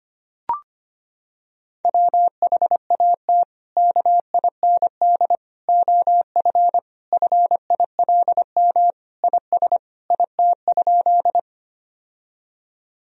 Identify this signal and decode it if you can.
Morse code, heard as a keyed tone, about 25 wpm, 700 Hz. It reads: WHAT KIND OF FILM IS IT?
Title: Sentences from Top 1000 Words - Review Part 3 - 25wpm